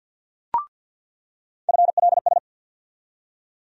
Morse code, heard as a keyed tone, about 53 wpm, 700 Hz. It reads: FLS
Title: USA Call Sign Postfixes - 3 Letters 50wpm